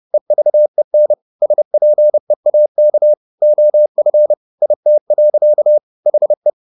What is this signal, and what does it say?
Morse code, heard as a keyed tone, about 30 wpm, 600 Hz. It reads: EVEN SPEAK OF IT. HE